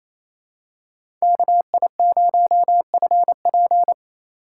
Morse code, heard as a keyed tone, about 28 wpm, 700 Hz. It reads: KI0FP
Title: USA Call Signs - 5 Characters - 2x Character Spacing - 28wpm